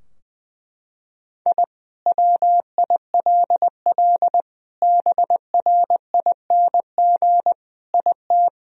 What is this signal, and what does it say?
Morse code, heard as a keyed tone, about 19 wpm, 700 Hz. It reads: I WILL BRING IT
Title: Sentences from Top 1000 Words - Review Part 3 - 20wpm